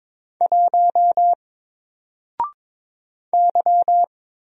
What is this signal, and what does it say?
Morse code, heard as a keyed tone, about 22 wpm, 700 Hz. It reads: 1 Y
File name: Single-Letter-Number-22wpm